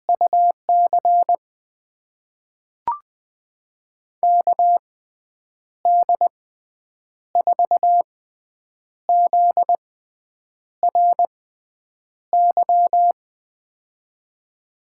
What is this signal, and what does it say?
Morse code, heard as a keyed tone, about 20 wpm, spaced as for 8 wpm, 700 Hz. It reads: UC KD4ZRY
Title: USA Call Signs - 6x Character Spacing - 20wpm